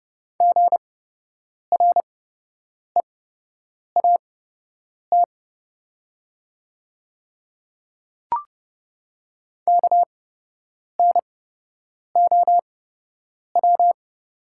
Morse code, heard as a keyed tone, about 30 wpm, spaced as for 8 wpm, 700 Hz. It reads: GREAT KNOW
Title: 8x Character Spacing - Top 500 Words - 30wpm